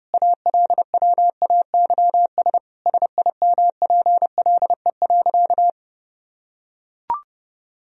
Morse code, heard as a keyed tone, about 32 wpm, 700 Hz. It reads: ALWAYS SIMPLE.
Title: Sentences from Top 1000 Words - Review Part 3 - 30wpm